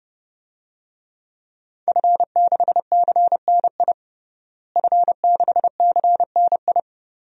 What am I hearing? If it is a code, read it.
Morse code, heard as a keyed tone, about 30 wpm, 700 Hz. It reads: F6CNI F6CNI
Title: DXCC-100 Call Signs - 5 Characters - Repeated 2x - 30wpm